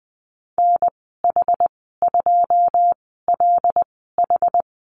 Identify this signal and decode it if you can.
Morse code, heard as a keyed tone, about 20 wpm, spaced as for 15 wpm, 700 Hz. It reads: NH2LH
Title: 088 Lesson 15g - Number 5 - Call Signs 20wpm